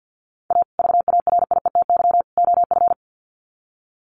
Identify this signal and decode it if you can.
Morse code, heard as a keyed tone, about 50 wpm, 700 Hz. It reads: A VARIETY OF
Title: Sets of 3 Words 50wpm